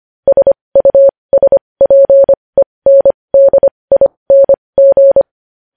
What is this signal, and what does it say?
Morse code, heard as a keyed tone, about 25 wpm, spaced as for 20 wpm, 550 Hz. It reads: SUSPENDING
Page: news-headlines-in-morse-code-at-20-wpm